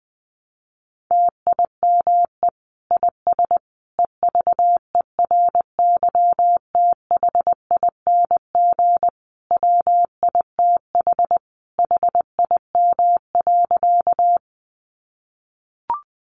Morse code, heard as a keyed tone, about 20 wpm, 700 Hz. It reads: TIME IS EVERYTHING WITH HIM.
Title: Sentences from Top 700 Words 20wpm